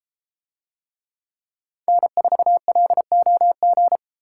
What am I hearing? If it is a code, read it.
Morse code, heard as a keyed tone, about 32 wpm, 700 Hz. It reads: N4LOG